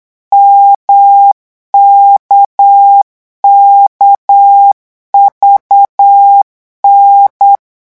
Morse code, heard as a keyed tone, about 8 wpm, 800 Hz. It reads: MKKVN
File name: morse mp3 / revision3-vitesse 8 mots